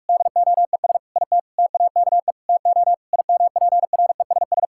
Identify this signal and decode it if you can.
Morse code, heard as a keyed tone, about 46 wpm, 700 Hz. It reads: DOES IT TAKE TO IMPRESS
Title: Sentences from Top 1600 Words 45wpm